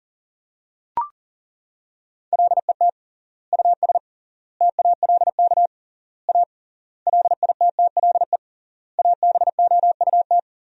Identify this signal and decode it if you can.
Morse code, heard as a keyed tone, about 40 wpm, 700 Hz. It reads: LET US TALK A LITTLE ABOUT
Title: 3x Word-Spacing - Sentences from Top 400 Words - 40wpm